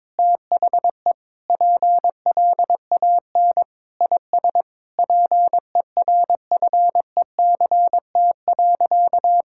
Morse code, heard as a keyed tone, about 22 wpm, 700 Hz. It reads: THE PLAN IS PERFECT.
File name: Sentences with 7-Word Phrases - 22wpm